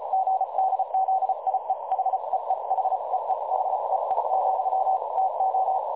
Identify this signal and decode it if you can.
Morse code, heard as a keyed tone, about 34 wpm, 760 Hz, with noise. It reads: GDZIE HEE HEE HEE, TO
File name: dlazaawansowanych / nagraniazpasma / websdr recording 3565.0kHz